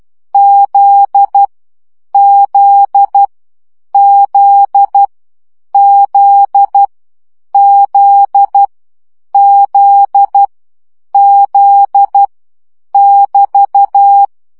Morse code, heard as a keyed tone, about 12 wpm, 800 Hz. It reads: Z Z Z Z Z Z Z <BT>
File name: CW gyakorlas z 10 baud